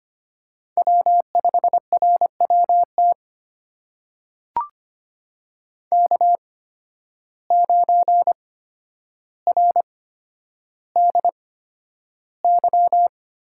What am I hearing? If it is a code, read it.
Morse code, heard as a keyed tone, about 25 wpm, spaced as for 11 wpm, 700 Hz. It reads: W5RWT K9RDY
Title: USA Call Signs - 5 Characters - 8x Character Spacing - 25wpm